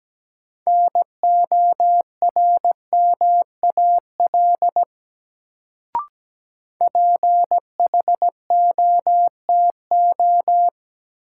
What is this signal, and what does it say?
Morse code, heard as a keyed tone, about 17 wpm, 700 Hz. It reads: NORMAL PHOTO
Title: Sentences from Top 1200 Words - New Word Drill - 17wpm